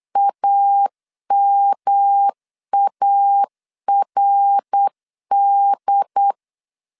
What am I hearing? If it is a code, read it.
Morse code, heard as a keyed tone, about 8 wpm, 800 Hz. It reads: AMARD